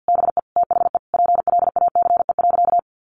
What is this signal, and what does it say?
Morse code, heard as a keyed tone, about 49 wpm, 700 Hz. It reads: BE THE PLACE.
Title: Sentences from Top 1300 Words 50wpm